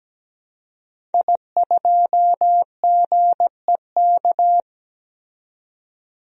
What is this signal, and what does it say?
Morse code, heard as a keyed tone, about 17 wpm, 700 Hz. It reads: I2GEK
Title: DXCC-100 Call Signs - 5 Characters - 17wpm